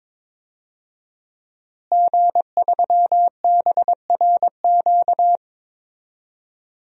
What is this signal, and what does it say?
Morse code, heard as a keyed tone, about 22 wpm, 700 Hz. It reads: G3BRQ
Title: DXCC-100 Call Signs - 5 Characters - Repeated 2x - 22wpm